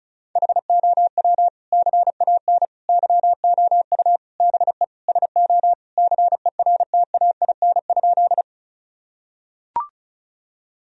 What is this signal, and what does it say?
Morse code, heard as a keyed tone, about 34 wpm, 700 Hz. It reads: HOW CAN YOU BE SO CERTAIN?
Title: Sentences from Top 1000 Words - Review Part 2 - 35wpm